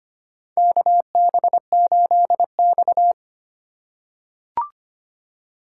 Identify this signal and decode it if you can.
Morse code, heard as a keyed tone, about 25 wpm, 700 Hz. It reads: KB8X
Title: USA Call Signs - 4 Characters - 4x Character Spacing - 25wpm